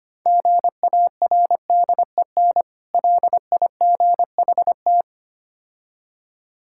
Morse code, heard as a keyed tone, about 25 wpm, 700 Hz. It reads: GARDEN LIGHT
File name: Easy Words - Speed-Racing with Hints - 25wpm